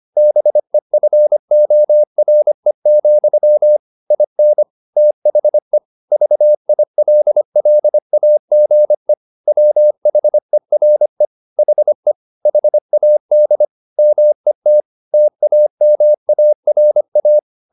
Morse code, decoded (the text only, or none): BEFORE, IN THE VILLAGE WHERE HE HAD MET TAMARA